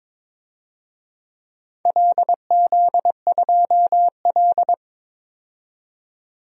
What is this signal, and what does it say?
Morse code, heard as a keyed tone, about 22 wpm, 700 Hz. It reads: LZ2L
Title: DXCC-100 Call Signs - 4 Characters - 22wpm